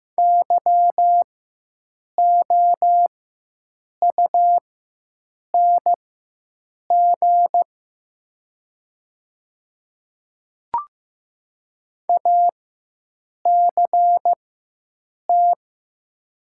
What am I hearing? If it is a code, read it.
Morse code, heard as a keyed tone, about 15 wpm, spaced as for 7 wpm, 700 Hz. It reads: YOUNG ACT
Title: 4x Character Spacing - Top 500 Words - 15wpm